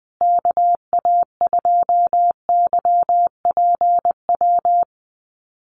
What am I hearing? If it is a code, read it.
Morse code, heard as a keyed tone, about 20 wpm, 700 Hz. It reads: KA2YPW
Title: USA Call Signs Repeated 3x 20wpm